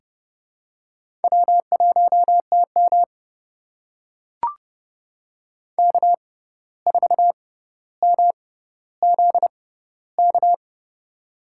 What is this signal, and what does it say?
Morse code, heard as a keyed tone, about 32 wpm, spaced as for 15 wpm, 700 Hz. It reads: W1TM K4MZK
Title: USA Call Signs - 6x Character Spacing - 30wpm